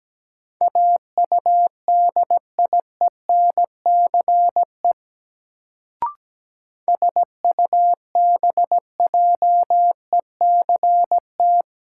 Morse code, heard as a keyed tone, about 17 wpm, 700 Hz. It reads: AUDIENCE SUBJECT
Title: Sentences from Top 800 Words - New Word Drill - 17wpm